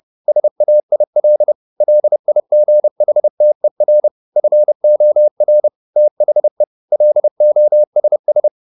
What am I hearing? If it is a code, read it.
Morse code, heard as a keyed tone, about 30 wpm, 600 Hz. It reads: SAIL LIGHTER FOR THE LOSS